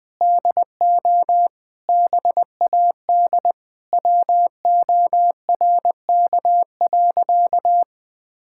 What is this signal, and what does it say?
Morse code, heard as a keyed tone, about 20 wpm, 700 Hz. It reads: DO BAD WORK.
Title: Sentences from Top 1000 Words - Review Part 6 - 20wpm